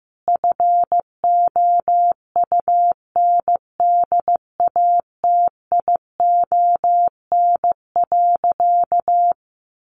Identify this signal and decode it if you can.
Morse code, heard as a keyed tone, about 15 wpm, 700 Hz. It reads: FOUNDATION.
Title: Sentences from Top 1200 Words 15wpm